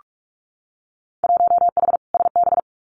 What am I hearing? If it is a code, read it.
Morse code, heard as a keyed tone, about 46 wpm, 700 Hz. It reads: 1H SB
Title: Winter Field Day Exchanges 45wpm